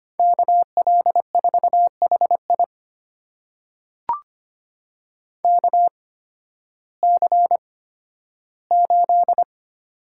Morse code, heard as a keyed tone, about 25 wpm, spaced as for 13 wpm, 700 Hz. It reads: KL4HI KC8
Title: USA Call Signs - 5 Characters - 8x Character Spacing - 25wpm